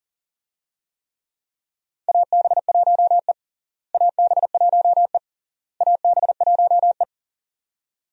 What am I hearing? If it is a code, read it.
Morse code, heard as a keyed tone, about 37 wpm, 700 Hz. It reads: AB1E AB1E AB1E
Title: USA Call Signs - 4 Characters - Repeated 3x - 40wpm